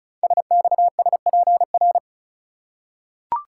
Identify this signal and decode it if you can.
Morse code, heard as a keyed tone, about 36 wpm, 700 Hz. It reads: SXSPR
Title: Code Groups - 5 Letters - 35wpm